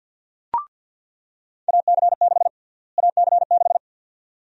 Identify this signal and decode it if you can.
Morse code, heard as a keyed tone, about 48 wpm, 700 Hz. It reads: AC6 AC6
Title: USA Call Sign Prefixes - Repeated 2x - 50wpm